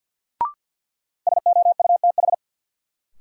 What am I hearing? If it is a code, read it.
Morse code, heard as a keyed tone, about 53 wpm, 700 Hz. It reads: SOUTH
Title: Sentences from Top 400 Words - New Word Drill - 50wpm